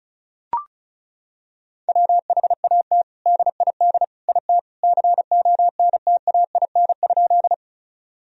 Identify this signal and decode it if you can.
Morse code, heard as a keyed tone, about 36 wpm, 700 Hz. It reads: WHAT DID IT CONTAIN?
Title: Sentences from Top 1000 Words - Review Part 2 - 35wpm